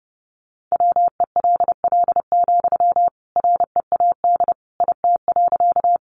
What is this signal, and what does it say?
Morse code, heard as a keyed tone, about 30 wpm, 700 Hz. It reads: WELL, READ IT.
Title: Sentences from Top 200 Words 30wpm